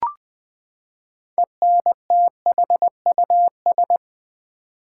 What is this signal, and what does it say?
Morse code, heard as a keyed tone, about 20 wpm, 700 Hz. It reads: ENTHUS
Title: Build A Word - 6 Letters At A Time - 12-Letter Words - 20wpm